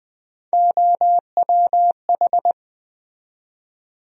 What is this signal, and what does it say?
Morse code, heard as a keyed tone, about 20 wpm, 700 Hz. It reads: OWH
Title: USA Call Sign Postfixes - 3 Letters 20wpm